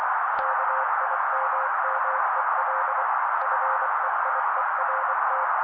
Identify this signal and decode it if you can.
Morse code, heard as a keyed tone, about 24 wpm, 530 Hz. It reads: KIMMEL FEIERT
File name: daily / 23wpm noise